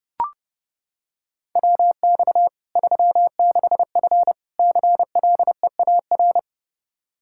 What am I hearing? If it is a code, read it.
Morse code, heard as a keyed tone, about 30 wpm, 700 Hz. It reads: WX 36F CLEAR
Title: Practice QSO - Exchange 2 - Info - 30wpm